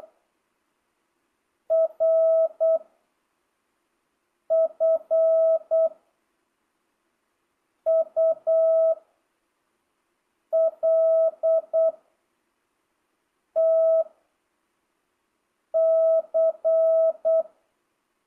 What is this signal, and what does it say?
Morse code, heard as a keyed tone, about 8 wpm, 660 Hz. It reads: R F U L T C
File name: jh8cbh / 2025cw